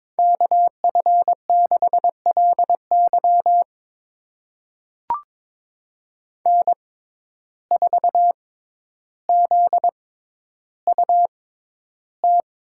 Morse code, heard as a keyed tone, about 23 wpm, spaced as for 11 wpm, 700 Hz. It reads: KF6LY N4ZUT
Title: USA Call Signs - 5 Characters - 6x Character Spacing - 22wpm